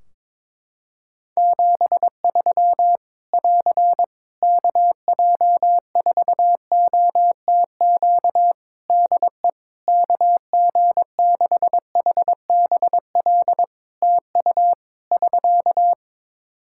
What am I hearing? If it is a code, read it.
Morse code, heard as a keyed tone, about 22 wpm, 700 Hz. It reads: 73 <AR> KJ4OTQ DE KG6HBL TU <SK>
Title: Practice QSO - Exchange 4 - Ending - 22wpm